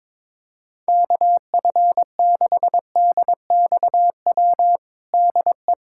Morse code, heard as a keyed tone, about 22 wpm, 700 Hz. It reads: KF6DXW DE
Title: USA Callsign DE Callsign - 6 Characters - 22wpm